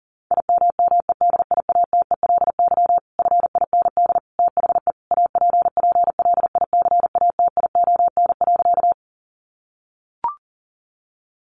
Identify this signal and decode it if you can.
Morse code, heard as a keyed tone, about 40 wpm, 700 Hz. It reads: IMMEDIATELY FIND THE APPLICATION.
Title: Sentences from Top 1200 Words 40wpm